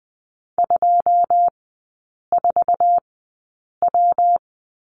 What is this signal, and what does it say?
Morse code, heard as a keyed tone, about 20 wpm, spaced as for 8 wpm, 700 Hz. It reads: 24W
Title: 121 Lesson 20e - Number 3 - Three Characters 20wpm